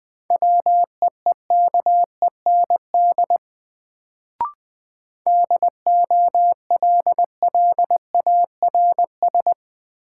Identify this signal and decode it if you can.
Morse code, heard as a keyed tone, about 20 wpm, 700 Hz. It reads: WEEKEND DOLLARS
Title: Sets of 5 Words - New Word Drill - 20wpm